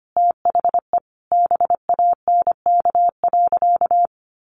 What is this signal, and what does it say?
Morse code, heard as a keyed tone, about 25 wpm, 700 Hz. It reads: THE BANK.
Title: Sentences from Top 1200 Words 25wpm